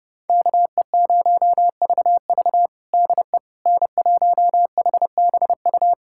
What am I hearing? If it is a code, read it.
Morse code, heard as a keyed tone, about 32 wpm, 700 Hz. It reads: KE0VV DE N1HBU
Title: USA Callsign DE Callsign - 5 Characters - 30wpm